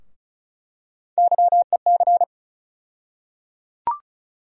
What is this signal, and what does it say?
Morse code, heard as a keyed tone, about 34 wpm, 700 Hz. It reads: YEC
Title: USA Call Sign Postfixes - 3 Letters 35wpm